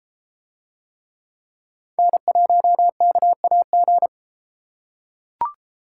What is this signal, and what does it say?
Morse code, heard as a keyed tone, about 33 wpm, 700 Hz. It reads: N1KAG